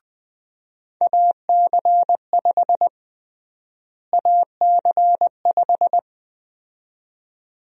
Morse code, heard as a keyed tone, about 21 wpm, 700 Hz. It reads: AC5 AC5
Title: USA Call Sign Prefixes - Repeated 2x - 20wpm